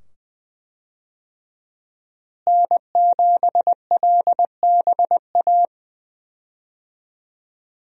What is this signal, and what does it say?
Morse code, heard as a keyed tone, about 20 wpm, 700 Hz. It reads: N7LBA
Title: USA Call Signs - 5 Characters - Repeated 2x - 20wpm